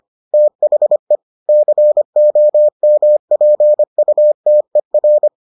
Morse code, heard as a keyed tone, about 25 wpm, 600 Hz. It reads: THE COMPUTER